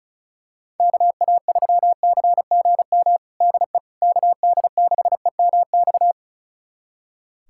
Morse code, heard as a keyed tone, about 34 wpm, 700 Hz. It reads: KA3CGM DE KD6EMX
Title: USA Callsign DE Callsign 35wpm